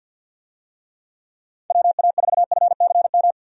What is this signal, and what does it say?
Morse code, heard as a keyed tone, about 51 wpm, 700 Hz. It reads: WA4FCM
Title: USA Call Signs - 6 Characters - 50wpm